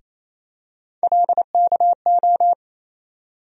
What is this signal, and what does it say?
Morse code, heard as a keyed tone, about 27 wpm, 700 Hz. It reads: LKO
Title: USA Call Sign Postfixes - 3 Letters 28wpm